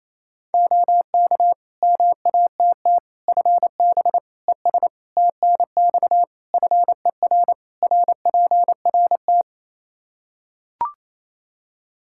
Morse code, heard as a keyed tone, about 28 wpm, 700 Hz. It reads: OK MATT FB ES TNX FER RPRT
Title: Practice QSO - Exchange 2 - Info - 28wpm